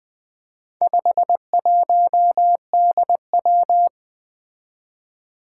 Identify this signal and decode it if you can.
Morse code, heard as a keyed tone, about 20 wpm, 700 Hz. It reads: 51DW
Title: Four Challenging Letter-Number Combinations 20wpm